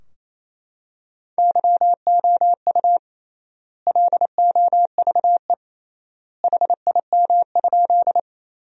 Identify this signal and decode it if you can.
Morse code, heard as a keyed tone, about 28 wpm, 700 Hz. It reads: YOU LOVE HIM?
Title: 3x Word-Spacing - Sentences from Top 400 Words - 28wpm